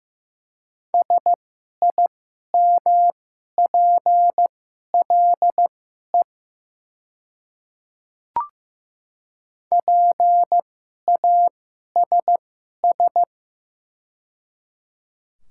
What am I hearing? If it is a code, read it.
Morse code, heard as a keyed tone, about 15 wpm, spaced as for 11 wpm, 700 Hz. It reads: SIMPLE PASS
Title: 2x Character Spacing - Top 500 Words - 15wpm